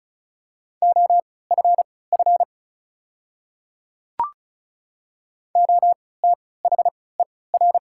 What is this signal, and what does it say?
Morse code, heard as a keyed tone, about 36 wpm, spaced as for 20 wpm, 690 Hz. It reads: OFF OTHER
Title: 3x Character Spacing - Top 200 Words - 35wpm